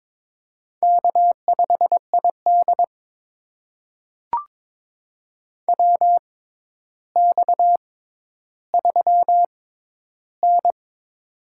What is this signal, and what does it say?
Morse code, heard as a keyed tone, about 21 wpm, spaced as for 11 wpm, 700 Hz. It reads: K5ID WX3N
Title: USA Call Signs - 4 Characters - 6x Character Spacing - 22wpm